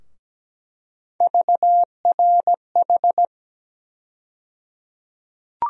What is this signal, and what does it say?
Morse code, heard as a keyed tone, about 17 wpm, 700 Hz. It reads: VRH
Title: USA Call Sign Postfixes - 3 Letters 17wpm